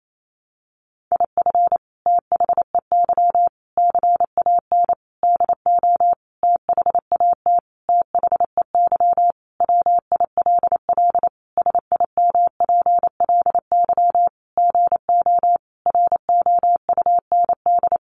Morse code, heard as a keyed tone, about 28 wpm, 700 Hz. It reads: IF THEY CAN DO THAT THEY WILL SIMPLY GO ROUND
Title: Sentences from Top 600 Words 28wpm